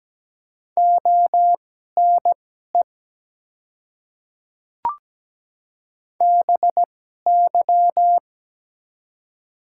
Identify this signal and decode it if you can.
Morse code, heard as a keyed tone, about 17 wpm, spaced as for 12 wpm, 700 Hz. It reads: ONE BY
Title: 2x Character Spacing - Top 100 Words - 17wpm